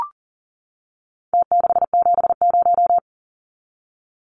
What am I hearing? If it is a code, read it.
Morse code, heard as a keyed tone, about 40 wpm, 700 Hz. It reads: T670
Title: Numbers - Contest Style - 4-Digit 40wpm